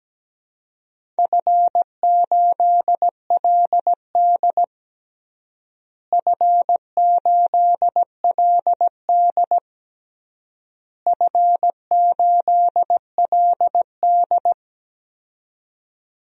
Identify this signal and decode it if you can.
Morse code, heard as a keyed tone, about 17 wpm, 700 Hz. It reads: F8LD F8LD F8LD
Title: DXCC-100 Call Signs - 4 Characters - Repeated 3x - 17wpm